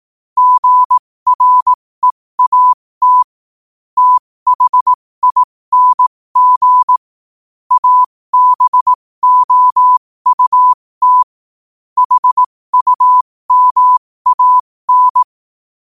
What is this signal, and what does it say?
Morse code, heard as a keyed tone, about 18 wpm, spaced as for 15 wpm, 1000 Hz. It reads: GREAT THING ABOUT HUMAN
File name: qotdfiles / 15wpm